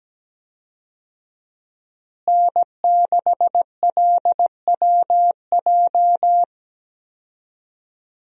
Morse code, heard as a keyed tone, about 17 wpm, 700 Hz. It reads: N6LWJ
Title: USA Call Signs - 5 Characters - Repeated 3x - 17wpm